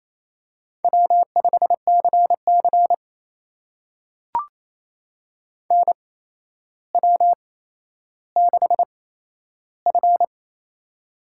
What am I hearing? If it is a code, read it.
Morse code, heard as a keyed tone, about 28 wpm, spaced as for 12 wpm, 700 Hz. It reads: W5CC NW6F
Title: USA Call Signs - 4 Characters - 8x Character Spacing - 28wpm